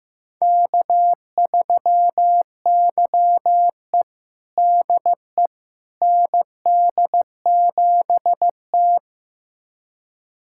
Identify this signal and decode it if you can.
Morse code, heard as a keyed tone, about 15 wpm, 700 Hz. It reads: K3YE DE ND7T
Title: USA Callsign DE Callsign - 4 Characters - 15wpm